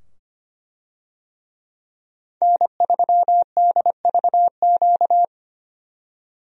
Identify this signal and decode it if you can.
Morse code, heard as a keyed tone, about 25 wpm, 700 Hz. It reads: N3DVQ
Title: USA Call Signs - 5 Characters - 25wpm